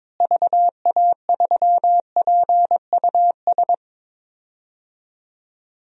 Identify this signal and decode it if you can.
Morse code, heard as a keyed tone, about 22 wpm, 700 Hz. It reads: VA3PUS
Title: DXCC-100 Call Signs - 6 Characters - Repeated 2x - 22wpm